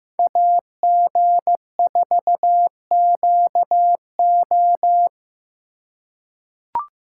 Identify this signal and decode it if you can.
Morse code, heard as a keyed tone, about 16 wpm, 700 Hz. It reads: AG4QO
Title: USA Call Signs - 5 Characters - 2x Character Spacing - 15wpm